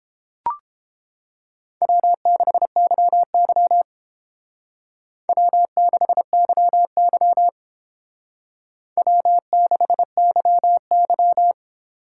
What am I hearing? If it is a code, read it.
Morse code, heard as a keyed tone, about 29 wpm, 700 Hz. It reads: W6YY W6YY W6YY